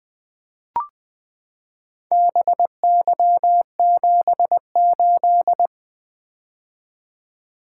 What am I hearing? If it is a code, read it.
Morse code, heard as a keyed tone, about 20 wpm, 700 Hz. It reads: BY78
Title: Four Letter-Number Combinations - 20wpm